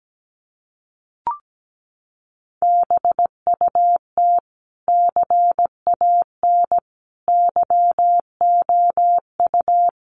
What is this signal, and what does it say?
Morse code, heard as a keyed tone, about 17 wpm, 700 Hz. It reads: BUT CAN YOU
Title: Sentences from Top 1400 Words 17wpm